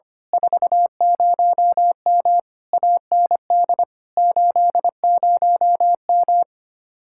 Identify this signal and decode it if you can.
Morse code, heard as a keyed tone, about 27 wpm, 700 Hz. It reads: 40M AND 80M
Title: Speed-Racing Common QSO Phrases - 25wpm